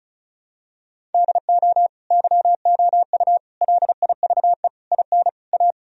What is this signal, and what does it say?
Morse code, heard as a keyed tone, about 36 wpm, 700 Hz. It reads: DO YOU LIVE IN A
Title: Sentences from Top 1000 Words - Review Part 1 - 35wpm